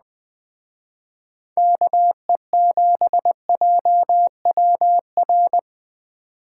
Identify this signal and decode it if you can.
Morse code, heard as a keyed tone, about 20 wpm, 700 Hz. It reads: KE7JWR
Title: Speed Racing - USA Call Signs 17wpm